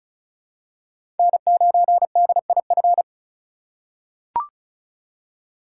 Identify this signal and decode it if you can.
Morse code, heard as a keyed tone, about 36 wpm, 700 Hz. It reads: N9DIF
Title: USA Call Signs - 5 Characters - 2x Character Spacing - 35wpm